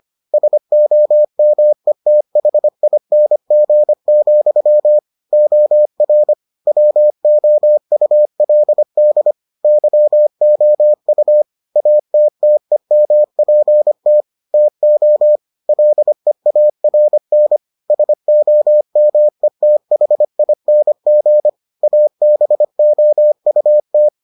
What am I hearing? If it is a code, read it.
Morse code, heard as a keyed tone, about 25 wpm, 600 Hz. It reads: SOMETHING, OR WOULD YOU ATTEMPT TO LEARN SOMETHING ABOUT